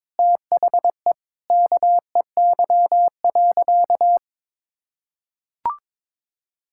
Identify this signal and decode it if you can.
Morse code, heard as a keyed tone, about 22 wpm, 700 Hz. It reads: THE KEY.
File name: practice / 22wpm / Sentences with 3-Word Phrases - 22wpm